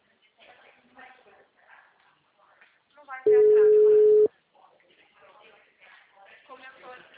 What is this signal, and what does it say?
Ringback tone (Europe, 425 Hz), with speech.
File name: out- -2021- - - .8973